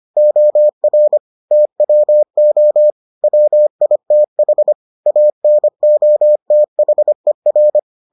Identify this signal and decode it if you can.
Morse code, heard as a keyed tone, about 25 wpm, 600 Hz. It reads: OR TWO WITH ANOTHER